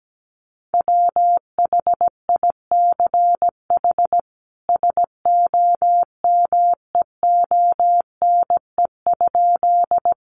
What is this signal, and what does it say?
Morse code, heard as a keyed tone, about 17 wpm, 700 Hz. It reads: WHICH SOMEONE?
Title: Sentences from Top 600 Words 17wpm